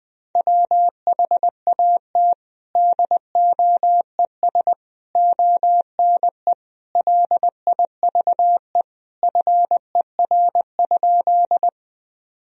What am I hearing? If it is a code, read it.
Morse code, heard as a keyed tone, about 20 wpm, 700 Hz. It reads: WHAT DOES ONE LIVE FER?
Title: Sentences from Top 200 Words and Abbreviations 20wpm